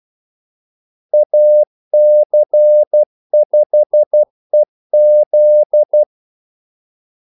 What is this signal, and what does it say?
Morse code, heard as a keyed tone, about 12 wpm, 600 Hz. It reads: AC5EZ
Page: 2 ekstra ordmellemrum mellem kaldesignalerne